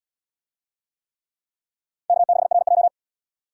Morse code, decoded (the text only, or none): K6RJ